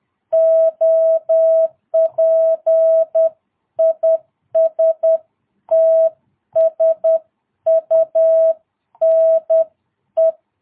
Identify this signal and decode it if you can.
Morse code, heard as a keyed tone, about 10 wpm, 650 Hz. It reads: OP IS TSUNE
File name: jh8cbh / 2022cw